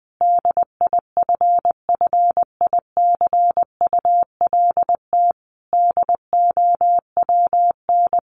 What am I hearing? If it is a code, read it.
Morse code, heard as a keyed tone, about 20 wpm, 700 Hz. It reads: DIFFICULT DOWN
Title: Sentences from Top 1100 Words 20wpm